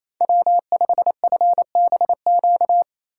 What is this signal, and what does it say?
Morse code, heard as a keyed tone, about 37 wpm, 700 Hz. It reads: W5FBQ